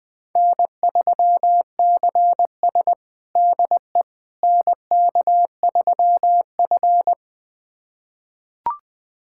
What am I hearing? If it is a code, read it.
Morse code, heard as a keyed tone, about 21 wpm, 700 Hz. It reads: N3CS DE NK3F
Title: USA Callsign DE Callsign - 4 Characters - 20wpm